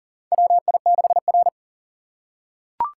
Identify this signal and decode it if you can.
Morse code, heard as a keyed tone, about 40 wpm, 700 Hz. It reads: WI6R
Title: USA Call Signs - 4 Characters - 4x Character Spacing - 40wpm